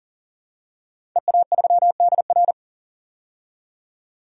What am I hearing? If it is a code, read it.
Morse code, heard as a keyed tone, about 40 wpm, 700 Hz. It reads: EA3DR
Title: DXCC-100 Call Signs - 5 Characters - 40wpm